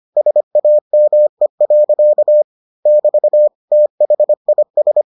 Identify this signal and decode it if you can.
Morse code, heard as a keyed tone, about 25 wpm, 600 Hz. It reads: SAME. <BT> THIS